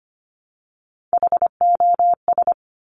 Morse code, heard as a keyed tone, about 25 wpm, 700 Hz. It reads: HOS
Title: Top Three Letter Combinations in Words 25wpm